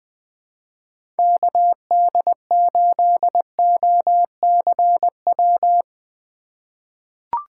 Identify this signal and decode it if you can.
Morse code, heard as a keyed tone, about 21 wpm, 700 Hz. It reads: KD8OCW
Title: 2023 POTA Hunter Call Signs - 20wpm